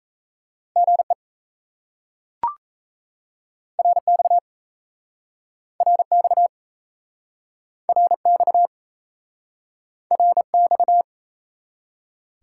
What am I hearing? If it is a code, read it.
Morse code, heard as a keyed tone, about 38 wpm, 700 Hz. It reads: GE RX RX RX RX